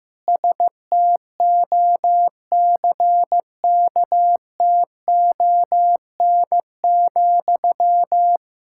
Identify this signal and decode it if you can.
Morse code, heard as a keyed tone, about 15 wpm, 700 Hz. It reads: STOCKTON,
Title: Practice QSO - Exchange 1 - Beginning - 15wpm